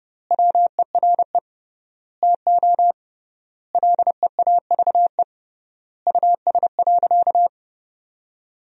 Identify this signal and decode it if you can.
Morse code, heard as a keyed tone, about 30 wpm, 700 Hz. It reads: WERE TO LEAVE US.
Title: 3x Word-Spacing - Sentences from Top 400 Words - 30wpm